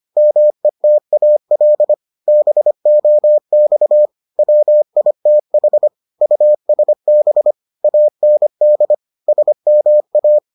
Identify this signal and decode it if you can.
Morse code, heard as a keyed tone, about 25 wpm, 600 Hz. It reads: METAL BOX WITH USB AND SMA